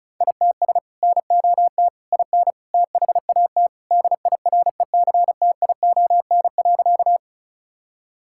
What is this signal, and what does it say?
Morse code, heard as a keyed tone, about 36 wpm, 700 Hz. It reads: ITS NOT IN THAT DIRECTION.
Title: Sentences from Top 1000 Words - Review Part 5 - 35wpm